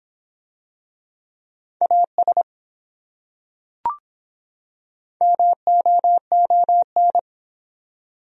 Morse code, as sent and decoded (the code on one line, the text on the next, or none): .- ... / -- --- --- -.
AS MOON